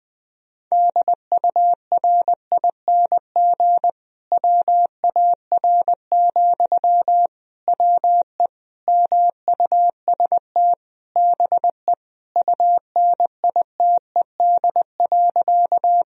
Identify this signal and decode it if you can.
Morse code, heard as a keyed tone, about 20 wpm, 700 Hz. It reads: DURING WAR, WE MUST BE UNITED.
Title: Sentences from Top 1000 Words - Review Part 4 - 20wpm